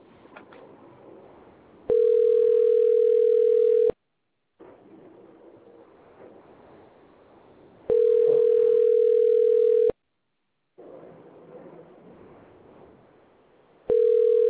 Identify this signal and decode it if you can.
Ringback tone (North America), 440 + 480 Hz.